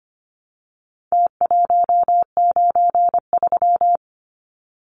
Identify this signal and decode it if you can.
Morse code, heard as a keyed tone, about 25 wpm, 700 Hz. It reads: T193
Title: Numbers - Contest Style - 4-Digit 25wpm